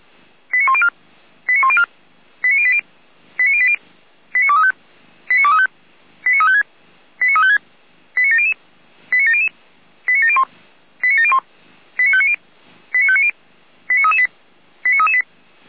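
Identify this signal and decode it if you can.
Five-tone selective call (ZVEI), heard as quick runs of five tones, tones of about 70 ms. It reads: 89104 89104 89080 89080 89235 89235 89356 89356 89700 89700 89701 89701 89590 89590 89338 89338
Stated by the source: ZVEI five-tone selective call received on SDR.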